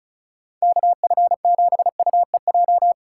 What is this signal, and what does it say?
Morse code, heard as a keyed tone, about 36 wpm, 700 Hz. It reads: KF7UEJ
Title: USA Call Signs - 6 Characters - 2x Character Spacing - 35wpm